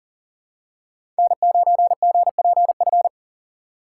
Morse code, heard as a keyed tone, about 40 wpm, 700 Hz. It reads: N9GPF
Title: USA Call Signs - 5 Characters - 2x Character Spacing - 40wpm